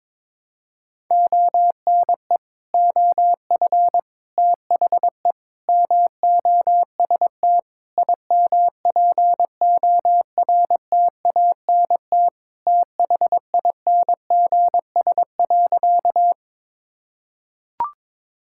Morse code, heard as a keyed tone, about 22 wpm, 700 Hz. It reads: ONE OF THE MOST IMPORTANT THINGS.
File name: practice / 22wpm / Sentences with 4-Word Phrases - 22wpm